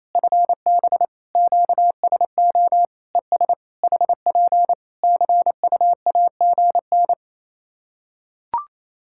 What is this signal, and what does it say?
Morse code, heard as a keyed tone, about 28 wpm, 700 Hz. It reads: FB QSO ES HP CUAGN
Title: Practice QSO - Exchange 4 - Ending - 28wpm